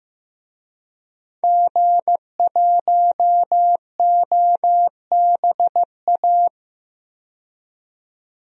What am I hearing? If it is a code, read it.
Morse code, heard as a keyed tone, about 15 wpm, 700 Hz. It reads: G1OBA
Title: DXCC-100 Call Signs - 5 Characters - 15wpm